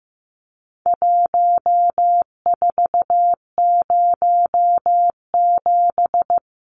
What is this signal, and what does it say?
Morse code, heard as a keyed tone, about 15 wpm, 700 Hz. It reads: 1407
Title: Numbers - Contest Style - 4-Digit 15wpm